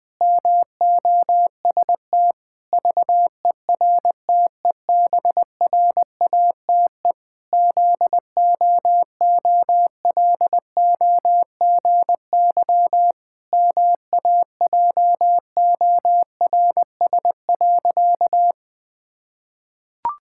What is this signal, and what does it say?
Morse code, heard as a keyed tone, about 20 wpm, 700 Hz. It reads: MOST VERTEBRATE ZOOLOGY MAJORS.
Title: Pangram Challenge v2 - 20wpm